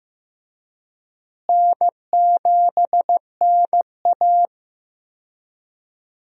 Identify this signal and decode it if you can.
Morse code, heard as a keyed tone, about 15 wpm, 700 Hz. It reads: N7NA
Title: USA Call Signs - 4 Characters - Repeated 3x - 15wpm